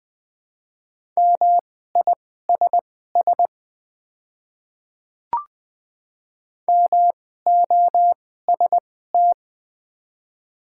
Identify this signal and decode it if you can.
Morse code, heard as a keyed tone, about 20 wpm, spaced as for 14 wpm, 700 Hz. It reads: MISS MOST
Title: 2x Character Spacing - Top 500 Words - 20wpm